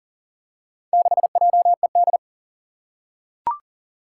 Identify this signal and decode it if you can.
Morse code, heard as a keyed tone, about 38 wpm, 700 Hz. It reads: 6JED
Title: Four Challenging Letter-Number Combinations 40wpm